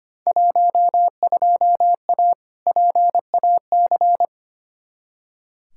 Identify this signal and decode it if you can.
Morse code, heard as a keyed tone, about 25 wpm, 700 Hz. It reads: 12A PAC
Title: Field Day Exchanges Encore - 25wpm